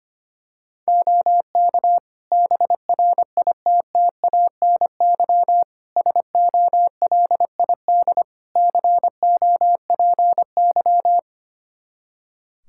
Morse code, heard as a keyed tone, about 25 wpm, 700 Hz. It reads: OK BRITTANY SOLID COPY
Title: K1USN SST QSO Practice 25wpm